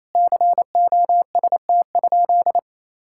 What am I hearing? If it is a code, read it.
Morse code, heard as a keyed tone, about 28 wpm, 700 Hz. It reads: COST?
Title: Sentences from Top 1000 Words - Review Part 6 - 28wpm